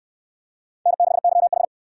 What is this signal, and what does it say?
Morse code, heard as a keyed tone, about 65 wpm, 700 Hz. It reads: N6YH